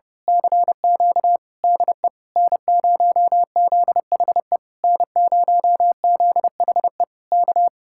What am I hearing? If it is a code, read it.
Morse code, heard as a keyed tone, about 28 wpm, 700 Hz. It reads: CQ DE N0ZHE N0ZHE K
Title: Speed-Racing Common QSO Phrases - 30wpm